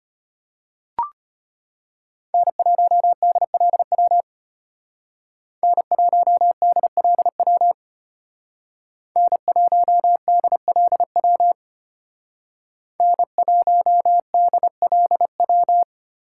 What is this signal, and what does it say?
Morse code, heard as a keyed tone, about 30 wpm, 700 Hz. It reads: N1DLW N1DLW N1DLW N1DLW
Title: Speed Racing - USA 1x3 Call Signs 25wpm